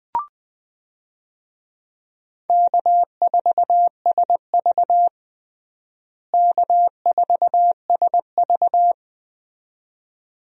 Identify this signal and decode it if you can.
Morse code, heard as a keyed tone, about 20 wpm, 700 Hz. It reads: K4SV K4SV
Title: USA Call Signs - 4 Characters - Repeated 2x - 20wpm